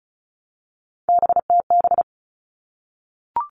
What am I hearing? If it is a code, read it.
Morse code, heard as a keyed tone, about 34 wpm, 700 Hz. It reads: BTB
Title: All Cut Numbers - 3-Digit 35wpm